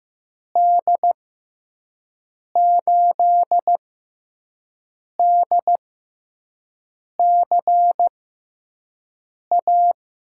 Morse code, heard as a keyed tone, about 15 wpm, spaced as for 5 wpm, 700 Hz. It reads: D8DCA
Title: USA Call Signs - 6 Characters - 6x Character Spacing - 15wpm